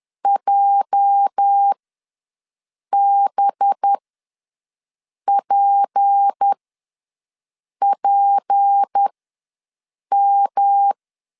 Morse code, heard as a keyed tone, about 10 wpm, spaced as for 5 wpm, 800 Hz. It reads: JBPPM
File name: LEZIONI / 9 28 ws 10wpm 4124